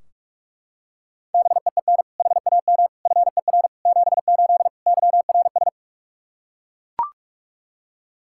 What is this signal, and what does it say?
Morse code, heard as a keyed tone, about 41 wpm, 700 Hz. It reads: BEEN HAM FER 78 YRS
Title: Practice QSO - Exchange 3 - More Info - 45wpm